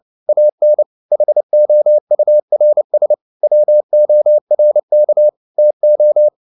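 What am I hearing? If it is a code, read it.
Morse code, heard as a keyed tone, about 29 wpm, 600 Hz. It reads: AN HOURS WORK TO
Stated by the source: Morse code at 29 wpm, sending Ken Follett's novel Never in English, from CW med Gnister.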